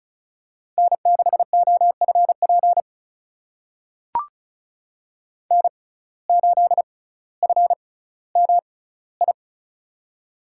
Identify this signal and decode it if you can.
Morse code, heard as a keyed tone, about 36 wpm, spaced as for 18 wpm, 700 Hz. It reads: N6OFP N8FMI
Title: USA Call Signs - 5 Characters - 6x Character Spacing - 35wpm